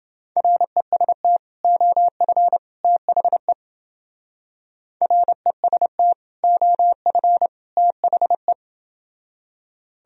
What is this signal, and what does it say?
Morse code, heard as a keyed tone, about 30 wpm, 700 Hz. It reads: REST OF THE REST OF THE